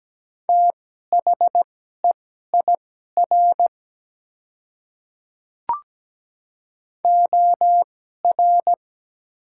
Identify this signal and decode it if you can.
Morse code, heard as a keyed tone, about 17 wpm, spaced as for 12 wpm, 700 Hz. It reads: THEIR OR
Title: 2x Character Spacing - Top 100 Words - 17wpm